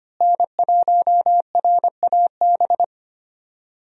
Morse code, heard as a keyed tone, about 25 wpm, 700 Hz. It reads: N1RAB
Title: USA Call Signs - 5 Characters - Repeated 2x - 25wpm